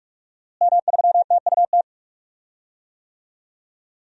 Morse code, heard as a keyed tone, about 44 wpm, 700 Hz. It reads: M3TUT
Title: DXCC-100 Call Signs - 5 Characters - Repeated 2x - 45wpm